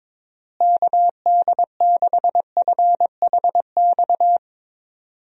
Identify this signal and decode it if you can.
Morse code, heard as a keyed tone, about 22 wpm, 700 Hz. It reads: KD6FHX
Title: USA Call Signs - 6 Characters - Repeated 2x - 22wpm